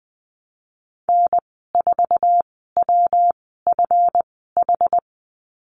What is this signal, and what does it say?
Morse code, heard as a keyed tone, about 19 wpm, spaced as for 14 wpm, 700 Hz. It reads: N4WFH
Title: 147 Lesson 24g - Letter y - Call Signs 20wpm